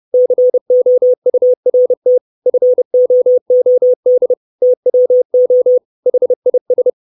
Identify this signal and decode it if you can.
Morse code, heard as a keyed tone, about 30 wpm, 500 Hz. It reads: COURT FOOD TWO HIS
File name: 632 Words 30WPM